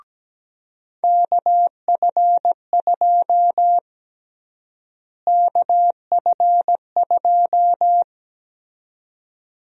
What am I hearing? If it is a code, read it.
Morse code, heard as a keyed tone, about 17 wpm, 700 Hz. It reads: KF2 KF2
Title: USA Call Sign Prefixes - Repeated 2x - 17wpm